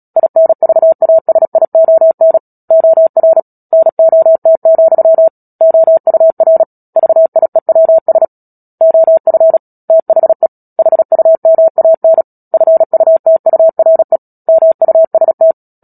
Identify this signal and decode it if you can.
Morse code, heard as a keyed tone, about 37 wpm, 650 Hz. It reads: INVASION OR NOT, OUR VIEWS OF THE HUMAN FUTURE MUST
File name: Book Two / 10-Chapter 10 - Epilogue - 36 WPM